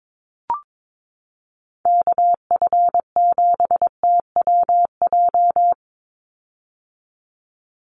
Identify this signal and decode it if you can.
Morse code, heard as a keyed tone, about 22 wpm, 700 Hz. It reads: KF7TWJ
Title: USA Call Signs 22wpm